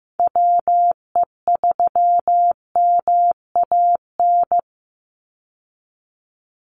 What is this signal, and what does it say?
Morse code, heard as a keyed tone, about 15 wpm, 700 Hz. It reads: WE3MAN
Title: USA Call Signs 15wpm